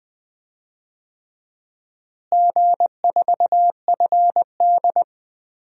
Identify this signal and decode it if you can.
Morse code, heard as a keyed tone, about 20 wpm, 700 Hz. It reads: G4FD
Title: DXCC-100 Call Signs - 4 Characters - Repeated 3x - 20wpm